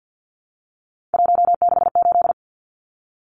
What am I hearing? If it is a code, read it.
Morse code, heard as a keyed tone, about 50 wpm, 700 Hz. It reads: 168
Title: Numbers - 3-Digit 50wpm